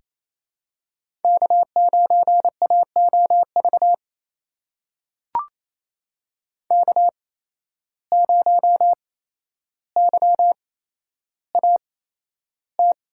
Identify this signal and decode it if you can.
Morse code, heard as a keyed tone, about 28 wpm, spaced as for 12 wpm, 700 Hz. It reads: K9AOV K0YAT
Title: USA Call Signs - 5 Characters - 8x Character Spacing - 28wpm